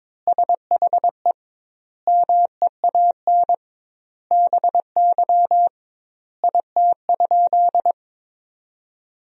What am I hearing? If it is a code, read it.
Morse code, heard as a keyed tone, about 22 wpm, 700 Hz. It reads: SHE MEAN BY IT?
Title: 2x Word-Spacing - Sentences from Top 200 Words - 22wpm